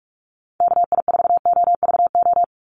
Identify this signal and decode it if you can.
Morse code, heard as a keyed tone, about 46 wpm, 690 Hz. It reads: KI4OVO
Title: USA Call Signs 45wpm